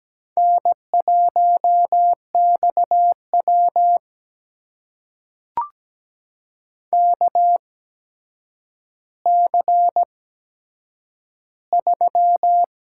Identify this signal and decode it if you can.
Morse code, heard as a keyed tone, about 17 wpm, spaced as for 8 wpm, 700 Hz. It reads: N1XW KC3
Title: USA Call Signs - 8x Character Spacing - 17wpm